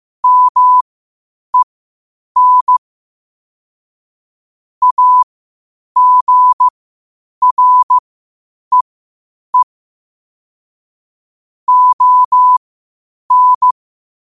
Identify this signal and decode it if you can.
Morse code, heard as a keyed tone, about 15 wpm, spaced as for 8 wpm, 1000 Hz. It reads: MEN AGREE ON
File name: qotdfiles / 8wpm